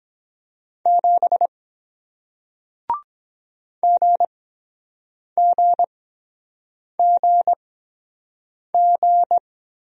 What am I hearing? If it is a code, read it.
Morse code, heard as a keyed tone, about 22 wpm, spaced as for 7 wpm, 700 Hz. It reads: 7 GGGG